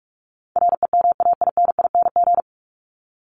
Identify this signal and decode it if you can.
Morse code, heard as a keyed tone, about 44 wpm, 700 Hz. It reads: REMAINING
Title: ING Suffix - Top 100 Words with Base Words 45wpm